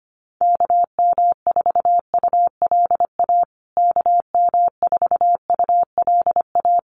Morse code, heard as a keyed tone, about 25 wpm, 700 Hz. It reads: KM4ULA KM4ULA
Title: USA Call Signs Repeated 2x 25wpm